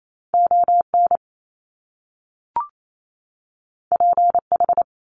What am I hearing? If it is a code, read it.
Morse code, heard as a keyed tone, about 28 wpm, 700 Hz. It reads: ON PH
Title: Top Two Letter Combinations in Words 28wpm